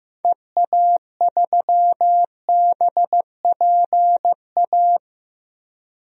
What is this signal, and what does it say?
Morse code, heard as a keyed tone, about 15 wpm, 700 Hz. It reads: EA3BPA
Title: DXCC-100 Call Signs - 6 Characters - Repeated 2x - 15wpm